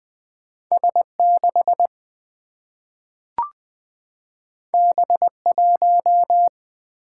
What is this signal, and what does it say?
Morse code, heard as a keyed tone, about 20 wpm, 700 Hz. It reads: S6 B1
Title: Single-Letter Single-Number 20wpm